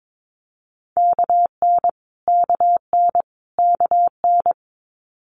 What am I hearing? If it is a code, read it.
Morse code, heard as a keyed tone, about 22 wpm, 700 Hz. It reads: KN KN KN
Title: Multi-letter Phonemes Repeated 3x 22wpm